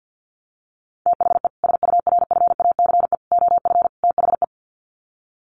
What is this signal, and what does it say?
Morse code, heard as a keyed tone, about 50 wpm, 700 Hz. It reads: THE SURFACE OF THE
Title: Sets of 4 Words 50wpm